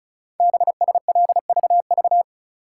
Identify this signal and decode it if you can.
Morse code, heard as a keyed tone, about 36 wpm, 690 Hz. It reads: BSLVV
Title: Code Groups - 5 Letters - 35wpm